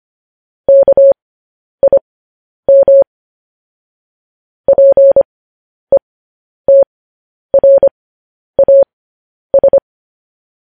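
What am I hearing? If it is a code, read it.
Morse code, heard as a keyed tone, about 25 wpm, spaced as for 10 wpm, 550 Hz. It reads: KIM PETRAS
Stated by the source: Morse code news headlines transmission.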